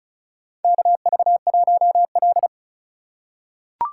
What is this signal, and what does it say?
Morse code, heard as a keyed tone, about 36 wpm, 700 Hz. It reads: KV1L
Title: USA Call Signs - 3x Character Spacing - 35wpm